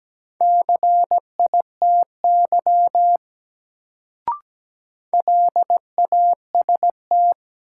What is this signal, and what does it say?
Morse code, heard as a keyed tone, about 17 wpm, 700 Hz. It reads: CITY LAST
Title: Sentences from Top 300 Words - New Word Drill - 17wpm